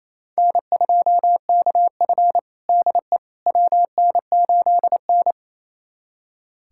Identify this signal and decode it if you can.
Morse code, heard as a keyed tone, about 28 wpm, 700 Hz. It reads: N2KF DE WN8N
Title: USA Callsign DE Callsign - 4 Characters - 28wpm